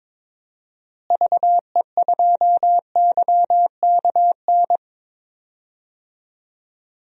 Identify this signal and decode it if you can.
Morse code, heard as a keyed tone, about 22 wpm, 700 Hz. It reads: VE2YKN
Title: DXCC-100 Call Signs - 6 Characters - Repeated 3x - 22wpm